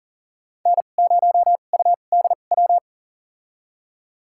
Morse code, heard as a keyed tone, about 40 wpm, spaced as for 29 wpm, 700 Hz. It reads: N0UDW
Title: USA Call Signs - 5 Characters - 2x Character Spacing - 40wpm